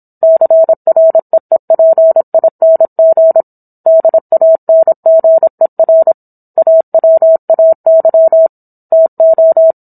Morse code, heard as a keyed tone, about 27 wpm, 650 Hz. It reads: CREEPING DANGER AWAY TO